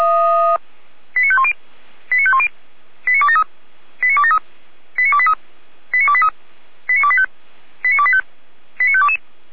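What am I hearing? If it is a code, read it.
Five-tone selective call (ZVEI), heard as quick runs of five tones, tones of about 70 ms. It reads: 89510 89510 89273 89273 89283 89283 89285 89285 89522
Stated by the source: ZVEI five-tone selective call 89078, received via SDR.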